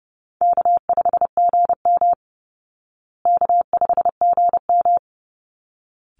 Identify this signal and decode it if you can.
Morse code, heard as a keyed tone, about 30 wpm, 700 Hz. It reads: K5GM K5GM
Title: Call Signs Repeated 3x - 30wpm